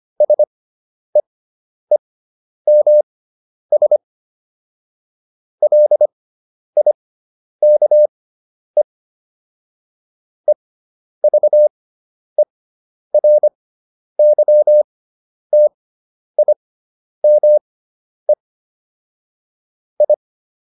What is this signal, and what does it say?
Morse code, heard as a keyed tone, about 25 wpm, spaced as for 10 wpm, 600 Hz. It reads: SEEMS LIKE EVERYTIME I